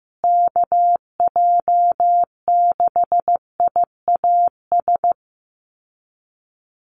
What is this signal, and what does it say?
Morse code, heard as a keyed tone, about 15 wpm, 700 Hz. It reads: KJ6IAS
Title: USA Call Signs Repeated 2x 15wpm